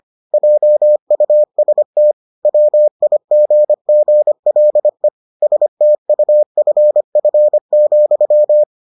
Morse code, heard as a keyed tone, about 25 wpm, 600 Hz. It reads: JUST WIGGLE STUFF,